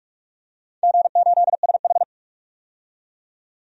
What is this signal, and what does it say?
Morse code, heard as a keyed tone, about 46 wpm, 700 Hz. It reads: G8SH
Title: DXCC-100 Call Signs - 4 Characters - 45wpm